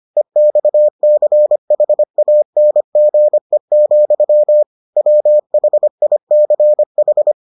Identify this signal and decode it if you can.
Morse code, heard as a keyed tone, about 25 wpm, 600 Hz. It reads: EXCHANGE, WHICH